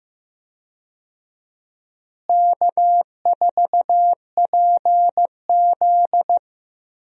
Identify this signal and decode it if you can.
Morse code, heard as a keyed tone, about 15 wpm, 700 Hz. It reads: K4PZ
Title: USA Call Signs - 4 Characters - Repeated 3x - 15wpm